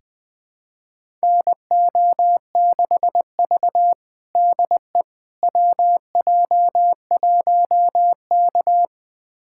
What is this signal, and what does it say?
Morse code, heard as a keyed tone, about 20 wpm, 700 Hz. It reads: NO6V DE WJ1K
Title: USA Callsign DE Callsign - 4 Characters - 20wpm